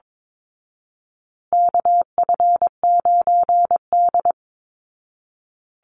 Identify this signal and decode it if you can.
Morse code, heard as a keyed tone, about 22 wpm, 700 Hz. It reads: KF9D
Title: Call Signs Repeated 2x - 22wpm